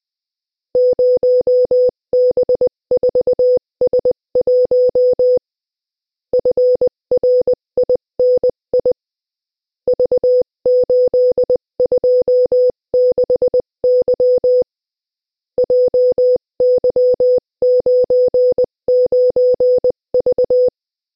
Morse code, heard as a keyed tone, about 20 wpm, 500 Hz. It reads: 0B4S1 FRINI V826Y JY99V
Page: Приём на слух сигналов кода Морзе - Тестовое задание 3